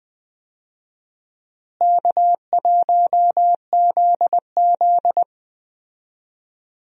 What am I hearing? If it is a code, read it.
Morse code, heard as a keyed tone, about 20 wpm, 700 Hz. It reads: K1ZZ
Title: USA Call Signs - 4 Characters - Repeated 2x - 20wpm